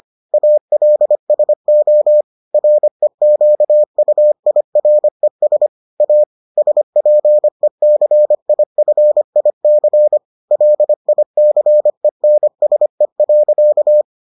Morse code, heard as a keyed tone, about 25 wpm, 600 Hz. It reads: ALSO REQUIRES A SPECIFIC LICENSE.